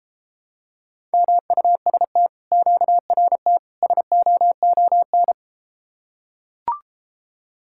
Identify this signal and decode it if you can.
Morse code, heard as a keyed tone, about 33 wpm, 700 Hz. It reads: MUST QRT SOON